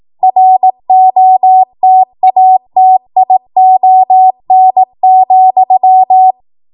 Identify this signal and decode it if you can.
Morse code, heard as a keyed tone, about 18 wpm, 750 Hz. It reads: ROTATION,
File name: Morse / 18WPM